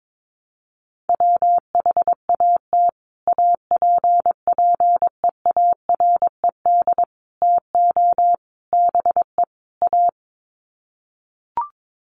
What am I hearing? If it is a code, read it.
Morse code, heard as a keyed tone, about 22 wpm, 700 Hz. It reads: WHAT APPEARED TO BE A
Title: Sets of 5 Words Encore2 22wpm